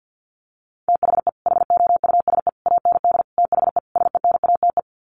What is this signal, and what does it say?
Morse code, heard as a keyed tone, about 48 wpm, 700 Hz. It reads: THE HOUSE AND THE SENATE
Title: Sets of 5 Words Encore2 50wpm